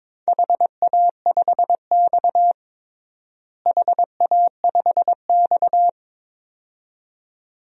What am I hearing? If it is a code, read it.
Morse code, heard as a keyed tone, about 22 wpm, 700 Hz. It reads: HA5X HA5X
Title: DXCC-100 Call Signs - 4 Characters - Repeated 2x - 22wpm